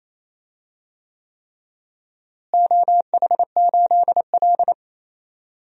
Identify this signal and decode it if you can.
Morse code, heard as a keyed tone, about 28 wpm, 700 Hz. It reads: OH8L
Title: DXCC-100 Call Signs - 4 Characters - Repeated 3x - 28wpm